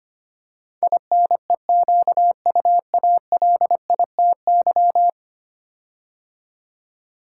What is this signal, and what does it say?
Morse code, heard as a keyed tone, about 25 wpm, 700 Hz. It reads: INEQUALITY
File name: 10 Letter Words - First 3 Letters Given - 25wpm